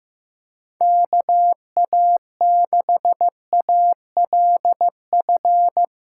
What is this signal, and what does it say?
Morse code, heard as a keyed tone, about 16 wpm, 700 Hz. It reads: KA6ALF
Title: Speed Racing - USA 2x3 Call Signs 15wpm